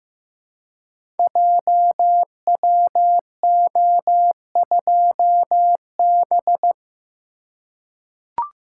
Morse code, heard as a keyed tone, about 15 wpm, 700 Hz. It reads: JWO2B
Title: Code Groups - 5 Letter:Numbers - 15wpm